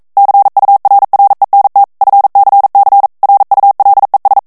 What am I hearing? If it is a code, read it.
Morse code, heard as a keyed tone, about 42 wpm, 800 Hz. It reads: CURRENT FCC RULES
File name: 40 mots minute